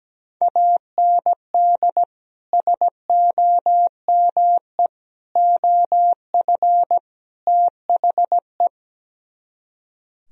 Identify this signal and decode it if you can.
Morse code, heard as a keyed tone, about 17 wpm, 700 Hz. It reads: AND SOME OF THE
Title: Speed-Racing - Sets of 4 Words 17wpm